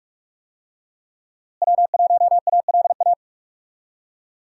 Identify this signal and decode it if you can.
Morse code, heard as a keyed tone, about 46 wpm, 700 Hz. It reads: W1ALA
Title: USA Call Signs - 5 Characters - Repeated 3x - 45wpm